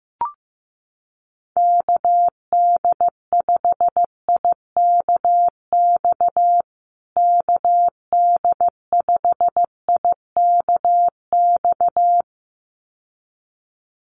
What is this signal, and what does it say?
Morse code, heard as a keyed tone, about 15 wpm, 700 Hz. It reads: KD5IKX KD5IKX
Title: USA Call Signs Repeated 2x 15wpm